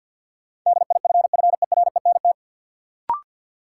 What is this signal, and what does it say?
Morse code, heard as a keyed tone, about 51 wpm, 700 Hz. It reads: DIFFERENT
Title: Sentences from Top 600 Words - New Word Drill - 50wpm